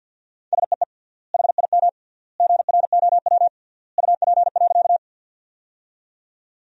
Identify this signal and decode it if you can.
Morse code, heard as a keyed tone, about 50 wpm, 700 Hz. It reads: SEE HIM GROW UP.
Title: 3x Word-Spacing - Sentences from Top 300 Words - 50wpm